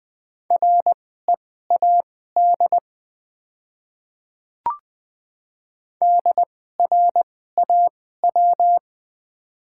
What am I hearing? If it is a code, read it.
Morse code, heard as a keyed tone, about 20 wpm, spaced as for 14 wpm, 700 Hz. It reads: READ DRAW
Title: 2x Character Spacing - Top 300 Words - 20wpm